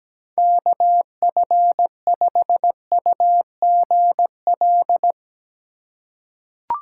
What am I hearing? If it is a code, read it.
Morse code, heard as a keyed tone, about 17 wpm, 700 Hz. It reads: KF5UGL
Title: USA Call Signs - 6 Characters - 2x Character Spacing - 17wpm